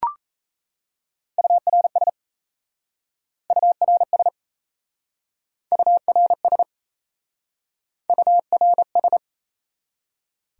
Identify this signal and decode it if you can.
Morse code, heard as a keyed tone, about 37 wpm, 700 Hz. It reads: URS URS URS URS